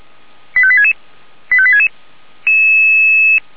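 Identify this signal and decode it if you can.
Five-tone selective call (ZVEI), heard as quick runs of five tones, tones of about 70 ms. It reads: 85866 85866
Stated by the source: ZVEI five-tone call received by SDR.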